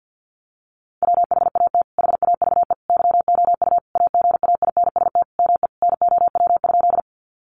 Morse code, heard as a keyed tone, about 49 wpm, 700 Hz. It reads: WHAT HAVE YOU AGAINST ME NOW?
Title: Sentences from Top 400 Words 50wpm